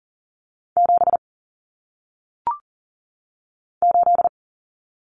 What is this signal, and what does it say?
Morse code, heard as a keyed tone, about 40 wpm, 700 Hz. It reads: 7 8
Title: Common Cut Numbers - 1-Digit 40wpm